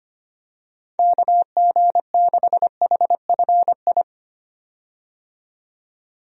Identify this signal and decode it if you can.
Morse code, heard as a keyed tone, about 25 wpm, 700 Hz. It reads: KG6HFI
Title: USA Call Signs - 6 Characters - 25wpm